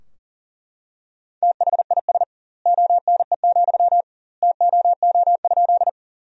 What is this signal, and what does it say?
Morse code, heard as a keyed tone, about 40 wpm, 700 Hz. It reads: THIS ONE, TOO?
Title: 2x Word-Spacing - Sentences from Top 200 Words - 40wpm